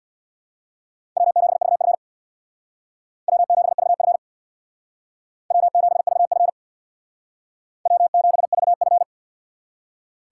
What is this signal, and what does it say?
Morse code, heard as a keyed tone, about 66 wpm, 700 Hz. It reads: W7VF W7VF W7VF W7VF